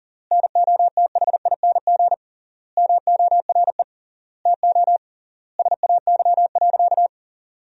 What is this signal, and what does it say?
Morse code, heard as a keyed tone, about 39 wpm, 700 Hz. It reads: NOTHING MORE TO SAY.
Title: 3x Word-Spacing - Sentences from Top 500 Words - 40wpm